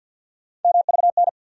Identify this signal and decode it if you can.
Morse code, heard as a keyed tone, about 52 wpm, 700 Hz. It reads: MVN
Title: USA Call Sign Postfixes - 3 Letters 50wpm